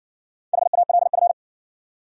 Morse code, heard as a keyed tone, about 60 wpm, 700 Hz. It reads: HALF